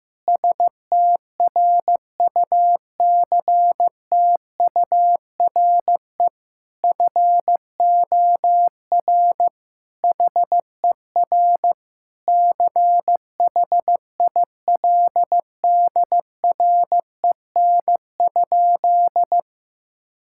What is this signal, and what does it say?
Morse code, heard as a keyed tone, about 16 wpm, 700 Hz. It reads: STRUCTURE FOR HER CHILDREN?
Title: Sentences from Top 1000 Words - Review Part 6 - 15wpm